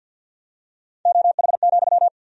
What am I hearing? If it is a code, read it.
Morse code, heard as a keyed tone, about 51 wpm, 700 Hz. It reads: OH,
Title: 2x Word-Spacing - Sentences from Top 500 Words - 50wpm